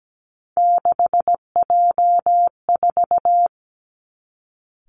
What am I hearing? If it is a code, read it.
Morse code, heard as a keyed tone, about 17 wpm, 700 Hz. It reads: 6J4
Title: Three Challenging Letter-Number Combinations 17wpm